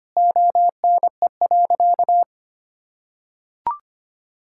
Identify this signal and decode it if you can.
Morse code, heard as a keyed tone, about 25 wpm, 700 Hz. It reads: ONE.
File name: practice / 25wpm / Sentences with 2-Word Phrases - 25wpm